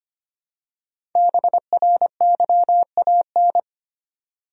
Morse code, heard as a keyed tone, about 25 wpm, 700 Hz. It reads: BRYAN
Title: USA Common Names - 25wpm